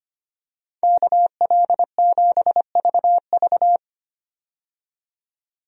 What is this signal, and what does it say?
Morse code, heard as a keyed tone, about 25 wpm, 700 Hz. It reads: KL7VV
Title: USA Call Signs - 5 Characters - 25wpm